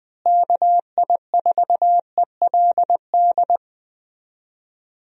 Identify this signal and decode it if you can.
Morse code, heard as a keyed tone, about 20 wpm, 700 Hz. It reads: KI4ELD
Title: USA Call Signs - 6 Characters - Repeated 3x - 20wpm